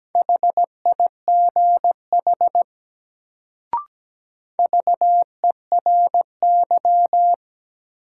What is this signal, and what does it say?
Morse code, heard as a keyed tone, about 17 wpm, 700 Hz. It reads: HIGH VERY
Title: Sets of 2 Words - New Word Drill - 17wpm